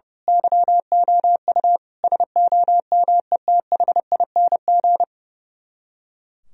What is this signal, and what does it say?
Morse code, heard as a keyed tone, about 29 wpm, 700 Hz. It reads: YOU SOMETHING
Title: Speed-Racing - Sets of 5 Words 30wpm